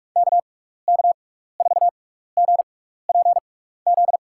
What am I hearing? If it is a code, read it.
Morse code, heard as a keyed tone, about 42 wpm, spaced as for 15 wpm, 700 Hz. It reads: KK4GPZ
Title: USA Call Signs - 6x Character Spacing - 45wpm